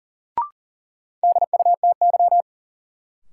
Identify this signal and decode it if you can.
Morse code, heard as a keyed tone, about 40 wpm, 700 Hz. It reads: DUTY
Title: Sentences from Top 1500 Words - New Word Drill Alt - 40wpm